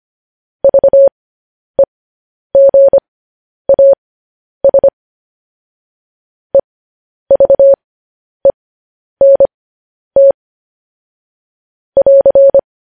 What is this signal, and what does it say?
Morse code, heard as a keyed tone, about 25 wpm, spaced as for 10 wpm, 550 Hz. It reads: VEGAS EVENT <AR>
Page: News Headlines in Morse Code at 10 WPM